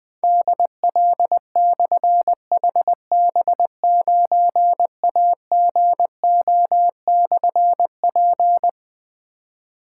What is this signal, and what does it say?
Morse code, heard as a keyed tone, about 20 wpm, 700 Hz. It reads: DL/HB9AGO/P
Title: Difficult Call Signs 20wpm